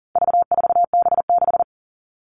Morse code, heard as a keyed tone, about 40 wpm, 700 Hz. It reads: V4B6
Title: All Cut Numbers - 4-Digit 40wpm